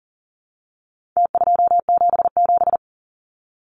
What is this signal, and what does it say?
Morse code, heard as a keyed tone, about 40 wpm, 700 Hz. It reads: T277
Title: Numbers - Contest Style - 4-Digit 40wpm